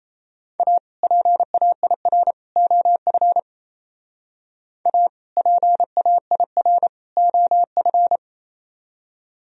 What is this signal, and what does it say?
Morse code, heard as a keyed tone, about 34 wpm, 700 Hz. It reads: A PAIR OF A PAIR OF
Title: Speed-Racing - Sets of 3 Words 28wpm